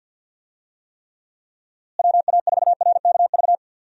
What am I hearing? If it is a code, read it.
Morse code, heard as a keyed tone, about 52 wpm, 700 Hz. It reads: WA4RKV
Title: USA Call Signs - 6 Characters - Repeated 3x - 50wpm